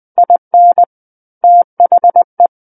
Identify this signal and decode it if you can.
Morse code, heard as a keyed tone, about 20 wpm, 700 Hz. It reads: IN THE
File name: 20 WPM